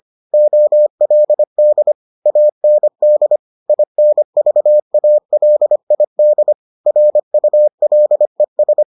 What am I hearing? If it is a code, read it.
Morse code, heard as a keyed tone, about 25 wpm, 600 Hz. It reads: OLD AND INVALID RULES